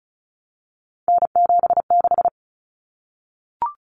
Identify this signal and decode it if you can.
Morse code, heard as a keyed tone, about 34 wpm, 700 Hz. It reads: N76
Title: Common Cut Numbers - 3-Digit 35wpm